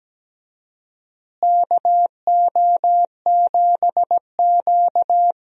Morse code, heard as a keyed tone, about 17 wpm, 700 Hz. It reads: KO7Q
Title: USA Call Signs - 4 Characters - Repeated 2x - 17wpm